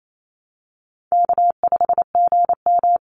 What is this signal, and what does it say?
Morse code, heard as a keyed tone, about 28 wpm, 700 Hz. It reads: K5GM
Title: Call Signs - 28wpm